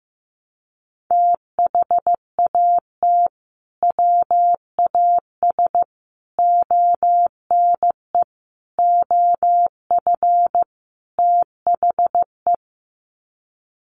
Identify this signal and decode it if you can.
Morse code, heard as a keyed tone, about 15 wpm, 700 Hz. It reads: THAT WAS ONE OF THE
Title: Sets of 5 Words 15wpm